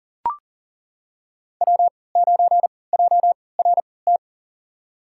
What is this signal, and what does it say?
Morse code, heard as a keyed tone, about 39 wpm, spaced as for 23 wpm, 700 Hz. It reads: W9JRT
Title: USA Call Signs - 5 Characters - 3x Character Spacing - 40wpm